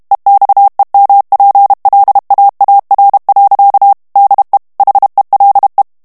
Morse code, heard as a keyed tone, about 32 wpm, 800 Hz. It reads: EXEMPLAAR. DE HELE
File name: HOLLANDAIS texte clair 2 vitesse 30 mots minute